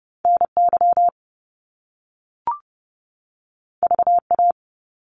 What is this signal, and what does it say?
Morse code, heard as a keyed tone, about 30 wpm, 700 Hz. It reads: NY VA
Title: US States 30wpm